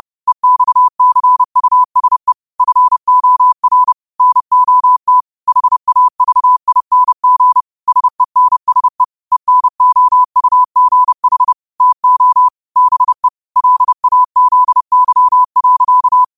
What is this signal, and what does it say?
Morse code, heard as a keyed tone, about 30 wpm, 1000 Hz. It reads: EXCUSE FOR NOT HAVING SENSE ENOUGH TO BE LAZY.
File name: qotdfiles / 30wpm